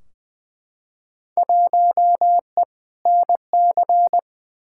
Morse code, heard as a keyed tone, about 20 wpm, 700 Hz. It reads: 1E NC
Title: Field Day Exchanges - 20wpm